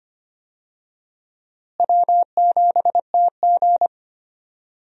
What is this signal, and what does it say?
Morse code, heard as a keyed tone, about 25 wpm, 700 Hz. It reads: W7TG
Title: USA Call Signs - 4 Characters - Repeated 2x - 25wpm